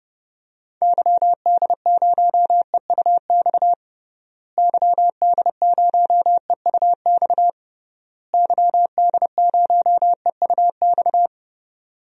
Morse code, heard as a keyed tone, about 30 wpm, 700 Hz. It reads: YD0EUX YD0EUX YD0EUX
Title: DXCC-100 Call Signs - 6 Characters - Repeated 3x - 30wpm